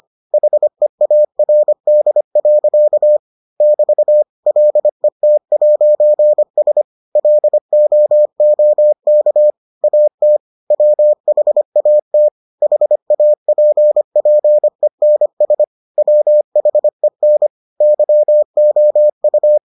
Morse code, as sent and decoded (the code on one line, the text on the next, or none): .... .- .-. -.. .-.-.- / -...- / .-.. - .----. ... / .-.. --- --- -.- / .- - / .-- .... .- - / .... .- .--. .--. -. ... / .-- .... -. / -.-- --- ..-
HEARD. <BT> LET'S LOOK AT WHAT HAPPENS WHEN YOU